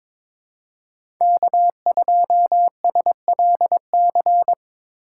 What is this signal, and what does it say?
Morse code, heard as a keyed tone, about 22 wpm, 700 Hz. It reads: K2SLC
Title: USA Call Signs - 5 Characters - 3x Character Spacing - 22wpm